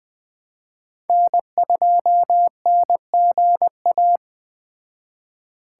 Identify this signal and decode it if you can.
Morse code, heard as a keyed tone, about 20 wpm, 700 Hz. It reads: N2NGA
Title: Speed Racing - USA 1x3 Call Signs 17wpm